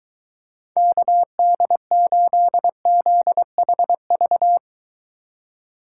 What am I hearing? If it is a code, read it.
Morse code, heard as a keyed tone, about 23 wpm, 700 Hz. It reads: KD8ZHV